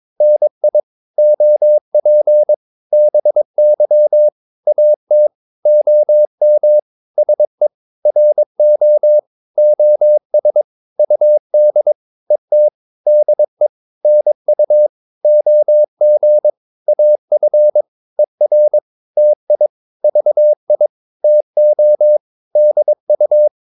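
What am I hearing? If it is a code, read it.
Morse code, heard as a keyed tone, about 22 wpm, 600 Hz. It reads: NI OP BY AT OM SE RO OS UD ET DE NU OG AF ER TI VI TO DU
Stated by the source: Morse practice of short Danish words at 22 wpm, from CW med Gnister.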